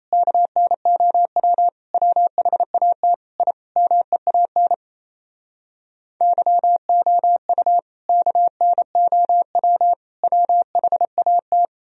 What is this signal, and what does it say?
Morse code, heard as a keyed tone, about 38 wpm, 700 Hz. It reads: KNOW WHAT I MEAN YOU KNOW WHAT